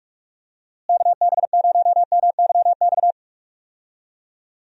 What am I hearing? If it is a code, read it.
Morse code, heard as a keyed tone, about 46 wpm, 700 Hz. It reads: KB0MYX
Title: USA Call Signs - 6 Characters - Repeated 3x - 45wpm